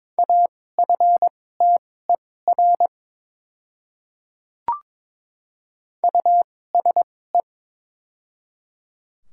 Morse code, heard as a keyed tone, about 22 wpm, spaced as for 16 wpm, 700 Hz. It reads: AFTER USE
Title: 2x Character Spacing - Top 500 Words - 22wpm